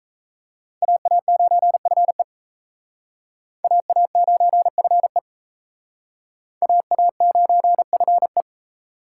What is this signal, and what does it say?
Morse code, heard as a keyed tone, about 35 wpm, 700 Hz. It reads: AA9FE AA9FE AA9FE